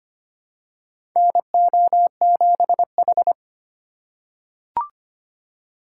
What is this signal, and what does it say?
Morse code, heard as a keyed tone, about 25 wpm, 700 Hz. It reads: NO7H
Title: USA Call Signs - 4 Characters - 2x Character Spacing - 25wpm